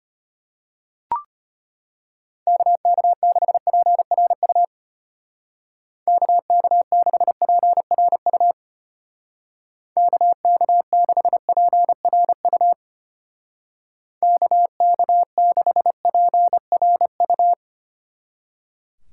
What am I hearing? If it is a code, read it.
Morse code, heard as a keyed tone, about 31 wpm, 700 Hz. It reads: KK6PRU KK6PRU KK6PRU KK6PRU
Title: Speed Racing - USA 2x3 Call Signs 25wpm